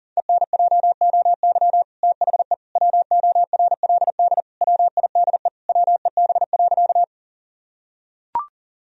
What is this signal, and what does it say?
Morse code, heard as a keyed tone, about 40 wpm, 700 Hz. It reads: ENJOY THE WORLD WIDE WEB.
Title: Sentences from Top 1000 Words - Review Part 4 - 40wpm